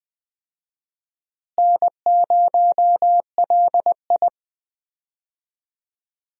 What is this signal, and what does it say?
Morse code, heard as a keyed tone, about 20 wpm, 700 Hz. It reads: N0LI
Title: USA Call Signs - 4 Characters - Repeated 3x - 20wpm